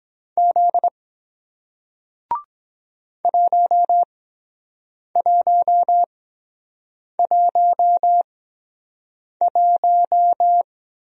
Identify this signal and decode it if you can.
Morse code, heard as a keyed tone, about 23 wpm, spaced as for 7 wpm, 700 Hz. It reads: Z 1111